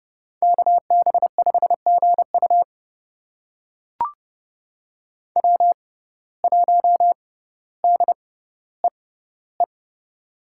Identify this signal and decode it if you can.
Morse code, heard as a keyed tone, about 30 wpm, spaced as for 15 wpm, 700 Hz. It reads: KB5GU W1DEE
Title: USA Call Signs - 5 Characters - 6x Character Spacing - 30wpm